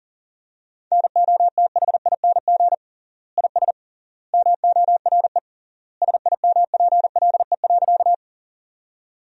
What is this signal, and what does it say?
Morse code, heard as a keyed tone, about 40 wpm, 700 Hz. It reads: NOTHING IS MORE SIMPLE.
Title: 3x Word-Spacing - Sentences from Top 500 Words - 40wpm